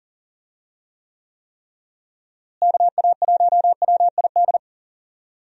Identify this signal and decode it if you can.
Morse code, heard as a keyed tone, about 40 wpm, 700 Hz. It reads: KA1WID
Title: USA Call Signs - 6 Characters - Repeated 3x - 40wpm